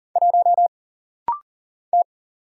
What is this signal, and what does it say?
Morse code, heard as a keyed tone, about 40 wpm, 700 Hz. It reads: 1 T
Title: Single Letter-Number Alt - Rapid-Fire 40wpm